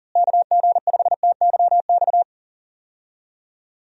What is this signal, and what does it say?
Morse code, heard as a keyed tone, about 37 wpm, 700 Hz. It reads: KG5TYX
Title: USA Call Signs - 6 Characters - Repeated 3x - 40wpm